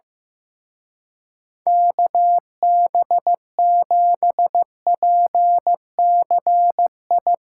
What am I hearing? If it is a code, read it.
Morse code, heard as a keyed tone, about 16 wpm, 700 Hz. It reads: KB7PCI
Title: Speed Racing - USA Call Signs 15wpm